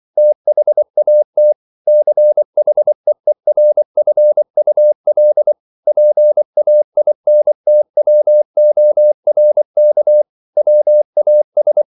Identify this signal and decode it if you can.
Morse code, heard as a keyed tone, about 24 wpm, 600 Hz. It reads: THAT CHEERFUL PAINTWORK WAS